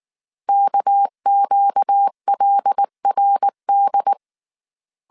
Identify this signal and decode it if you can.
Morse code, heard as a keyed tone, about 19 wpm, 800 Hz. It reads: KQLRD